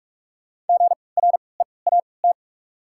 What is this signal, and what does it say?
Morse code, heard as a keyed tone, about 47 wpm, spaced as for 26 wpm, 700 Hz. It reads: GREAT
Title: 3x Character Spacing - Top 300 Words - 45wpm